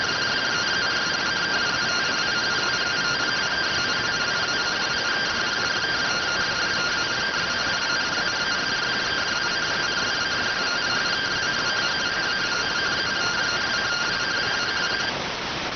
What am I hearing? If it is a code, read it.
Two-tone teleprinter signal, about 170 Hz shift.